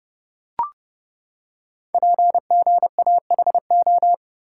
Morse code, heard as a keyed tone, about 30 wpm, 700 Hz. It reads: PGAHO
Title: Code Groups - 5 Letters - 30wpm